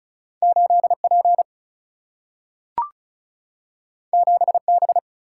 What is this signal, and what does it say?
Morse code, heard as a keyed tone, about 36 wpm, 700 Hz. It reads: 8P 7B
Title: Single-Letter Single-Number 35wpm